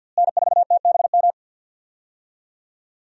Morse code, heard as a keyed tone, about 46 wpm, 700 Hz. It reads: N4TBM
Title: USA Call Signs - 5 Characters - Repeated 3x - 50wpm